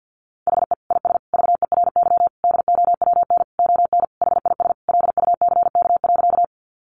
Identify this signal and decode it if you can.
Morse code, heard as a keyed tone, about 50 wpm, 700 Hz. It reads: HE IS VERY DOWN ON HIS LUCK.
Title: Sentences from Top 1500 Words 50wpm